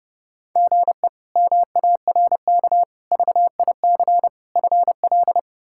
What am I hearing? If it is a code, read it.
Morse code, heard as a keyed tone, about 30 wpm, 700 Hz. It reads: GE MARK VIC FL
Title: K1USN SST QSO Practice - 30wpm